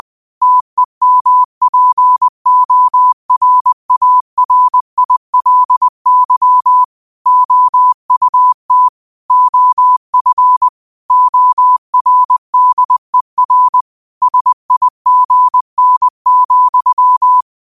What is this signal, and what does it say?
Morse code, heard as a keyed tone, about 20 wpm, 1000 Hz. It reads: TEMPORARILY OUT OF ORDER SIGN,
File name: qotdfiles / 20wpm